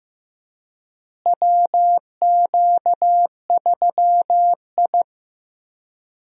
Morse code, heard as a keyed tone, about 15 wpm, 700 Hz. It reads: WQ3I
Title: USA Call Signs - 4 Characters - Repeated 2x - 15wpm